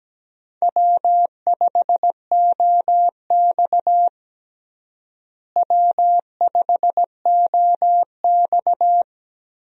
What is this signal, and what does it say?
Morse code, heard as a keyed tone, about 17 wpm, 700 Hz. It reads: W5OX W5OX
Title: USA Call Signs - 4 Characters - Repeated 2x - 17wpm